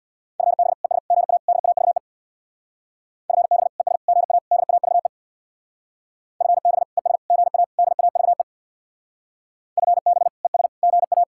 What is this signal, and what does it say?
Morse code, heard as a keyed tone, about 66 wpm, 700 Hz. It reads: FB ES GA DAVE FB ES GA DAVE FB ES GA DAVE FB ES GA